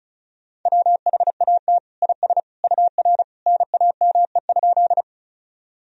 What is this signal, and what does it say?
Morse code, heard as a keyed tone, about 34 wpm, 690 Hz. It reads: WHAT IS UR NAME?
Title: Sentences from Top 200 Words and Abbreviations 35wpm